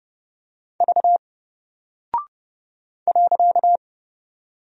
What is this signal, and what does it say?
Morse code, heard as a keyed tone, about 30 wpm, 700 Hz. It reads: V .
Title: Single Character (Letter:Number:Punctuation:Prosign) Alt 30wpm